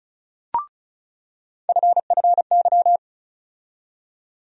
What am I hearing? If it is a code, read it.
Morse code, heard as a keyed tone, about 34 wpm, 700 Hz. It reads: FFY
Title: USA Call Sign Postfixes - 3 Letters 35wpm